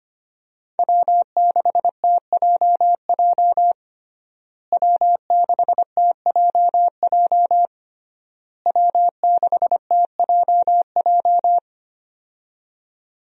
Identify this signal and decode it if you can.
Morse code, heard as a keyed tone, about 25 wpm, 700 Hz. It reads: W6TJJ W6TJJ W6TJJ
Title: USA Call Signs - 5 Characters - Repeated 3x - 25wpm